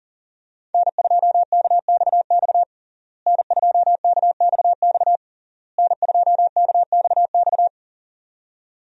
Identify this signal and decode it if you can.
Morse code, heard as a keyed tone, about 40 wpm, 700 Hz. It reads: N2KXX N2KXX N2KXX
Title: USA Call Signs - 5 Characters - Repeated 3x - 40wpm